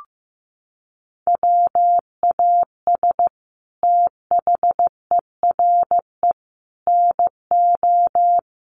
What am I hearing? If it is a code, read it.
Morse code, heard as a keyed tone, about 15 wpm, 700 Hz. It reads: WAS THERE NO
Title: Sentences from Top 1400 Words 15wpm